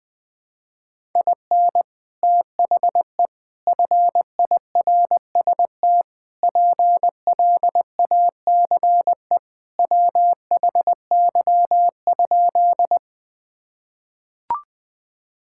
Morse code, heard as a keyed tone, about 20 wpm, 700 Hz. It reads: IN THE FIRST PLACE WHY?
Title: Sentences with 4-Word Phrases - 20wpm